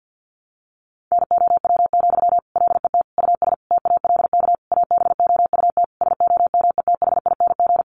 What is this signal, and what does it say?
Morse code, heard as a keyed tone, about 50 wpm, 700 Hz. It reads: NOW, LET US TALK ABOUT SOMETHING
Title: Sentences from Top 800 Words 50wpm